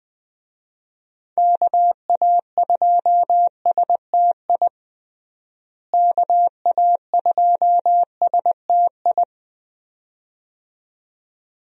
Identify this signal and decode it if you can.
Morse code, heard as a keyed tone, about 21 wpm, 700 Hz. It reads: KA2STI KA2STI
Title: USA Call Signs - 6 Characters - Repeated 2x - 20wpm